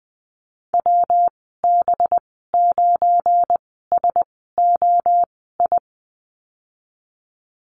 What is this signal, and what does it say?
Morse code, heard as a keyed tone, about 21 wpm, spaced as for 15 wpm, 700 Hz. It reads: WB9SOI
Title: 186 Lesson 30g - Letter b - Call Signs 20wpm